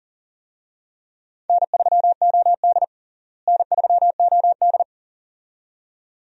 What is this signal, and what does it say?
Morse code, heard as a keyed tone, about 40 wpm, 700 Hz. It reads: N3OD N3OD
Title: USA Call Signs - 4 Characters - Repeated 2x - 40wpm